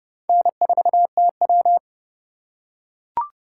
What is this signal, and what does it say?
Morse code, heard as a keyed tone, about 26 wpm, 700 Hz. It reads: N4TW